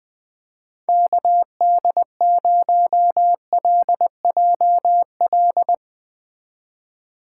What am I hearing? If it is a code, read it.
Morse code, heard as a keyed tone, about 20 wpm, 700 Hz. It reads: KD0LJL
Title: USA Call Signs - 4x Character Spacing - 20wpm